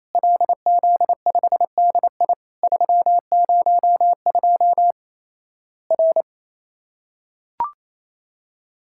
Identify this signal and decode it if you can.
Morse code, heard as a keyed tone, about 28 wpm, 700 Hz. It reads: LZ5DI 302 R
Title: Worked All Europe DX Contest - QTC Exchange - 28wpm